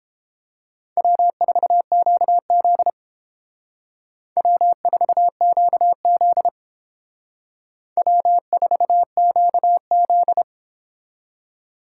Morse code, heard as a keyed tone, about 29 wpm, 690 Hz. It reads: W4QZ W4QZ W4QZ